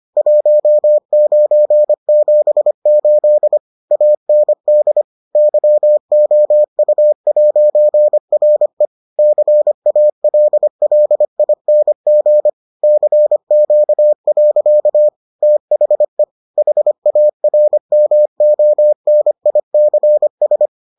Morse code, heard as a keyed tone, about 25 wpm, 600 Hz. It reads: 1978 AND YOU'RE CALLING CQ. THE HARMONICS